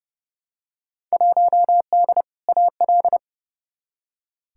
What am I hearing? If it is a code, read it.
Morse code, heard as a keyed tone, about 32 wpm, 700 Hz. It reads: 1D AL
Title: Field Day Exchanges - 30wpm